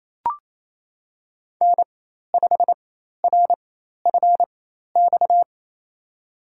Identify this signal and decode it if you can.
Morse code, heard as a keyed tone, about 28 wpm, spaced as for 13 wpm, 700 Hz. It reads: N5RFX
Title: USA Call Signs - 5 Characters - 4x Character Spacing - 28wpm